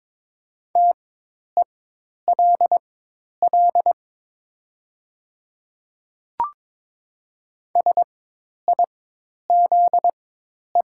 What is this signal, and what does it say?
Morse code, heard as a keyed tone, about 22 wpm, spaced as for 10 wpm, 700 Hz. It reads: TELL SIZE
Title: 4x Character Spacing - Top 500 Words - 22wpm